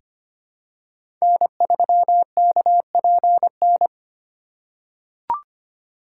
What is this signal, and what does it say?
Morse code, heard as a keyed tone, about 25 wpm, 700 Hz. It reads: N3KPN
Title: USA Call Signs - 4x Character Spacing - 25wpm